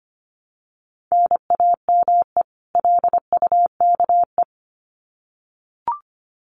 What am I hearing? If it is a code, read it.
Morse code, heard as a keyed tone, about 25 wpm, 700 Hz. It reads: NAME LUKE
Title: Common QSO Phrases 25wpm